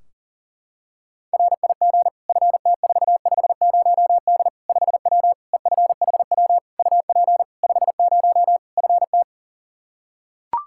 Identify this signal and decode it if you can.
Morse code, heard as a keyed tone, about 40 wpm, 700 Hz. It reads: RIG FT450D 5W EFHW UP 50 FT
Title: Practice QSO - Exchange 2 - Info - 40wpm